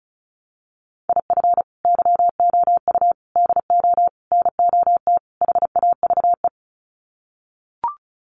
Morse code, heard as a keyed tone, about 34 wpm, 700 Hz. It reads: IF YOU DO NOT HAVE
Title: Sets of 5 Words 35wpm